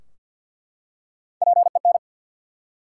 Morse code, heard as a keyed tone, about 50 wpm, 700 Hz. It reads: PEN
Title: Three Letter Words - 50wpm